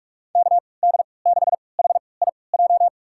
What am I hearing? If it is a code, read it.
Morse code, heard as a keyed tone, about 46 wpm, spaced as for 26 wpm, 700 Hz. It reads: KD6HIJ
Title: USA Call Signs - 3x Character Spacing - 45wpm